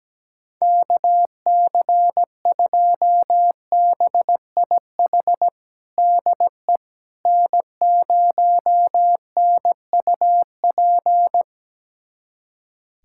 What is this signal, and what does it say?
Morse code, heard as a keyed tone, about 17 wpm, 700 Hz. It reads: KC2BIH DE N0NUP
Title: USA Callsign DE Callsign 17wpm